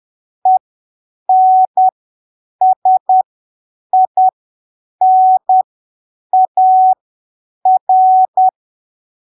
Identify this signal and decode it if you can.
Morse code, heard as a keyed tone, about 10 wpm, spaced as for 7 wpm, 750 Hz. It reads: ENSINAR
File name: arquivos / texto / texto 21 10ppm